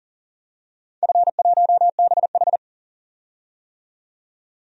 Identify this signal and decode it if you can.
Morse code, heard as a keyed tone, about 40 wpm, 700 Hz. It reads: F1BH
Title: DXCC-100 Call Signs - 4 Characters - 40wpm